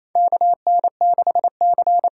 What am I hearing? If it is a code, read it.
Morse code, heard as a keyed tone, about 28 wpm, 700 Hz. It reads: KN6C
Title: USA Call Signs - 6 Characters - 8x Character Spacing - 28wpm